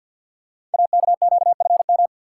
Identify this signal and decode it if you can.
Morse code, heard as a keyed tone, about 51 wpm, 690 Hz. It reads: AKQFM
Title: Code Groups - 5 Letter:Numbers - 50wpm